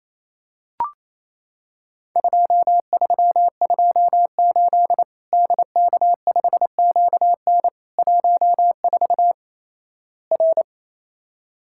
Morse code, heard as a keyed tone, about 28 wpm, 700 Hz. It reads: 2328 DK5QN 14 R
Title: Worked All Europe DX Contest - QTC Exchange - 28wpm